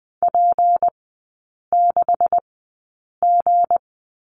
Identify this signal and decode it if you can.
Morse code, heard as a keyed tone, about 20 wpm, spaced as for 8 wpm, 700 Hz. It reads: P6G
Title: 198 Lesson 32e - Letter k - Three Characters 20wpm